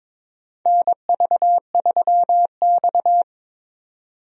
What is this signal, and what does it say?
Morse code, heard as a keyed tone, about 22 wpm, 700 Hz. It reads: NV3X
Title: USA Call Signs - 4 Characters - 22wpm